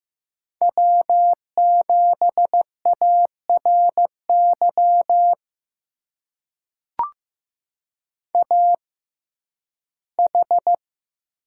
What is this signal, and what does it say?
Morse code, heard as a keyed tone, about 15 wpm, spaced as for 11 wpm, 700 Hz. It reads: W7ARY AH
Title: USA Call Signs - 6x Character Spacing - 15wpm